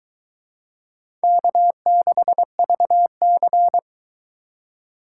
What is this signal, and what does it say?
Morse code, heard as a keyed tone, about 22 wpm, 700 Hz. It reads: K6VC